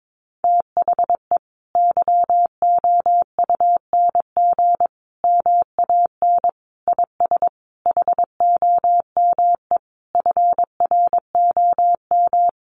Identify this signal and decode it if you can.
Morse code, heard as a keyed tone, about 22 wpm, 700 Hz. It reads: THE YOUNG MAN IS HOME FROM
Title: Sentences from Top 1300 Words 22wpm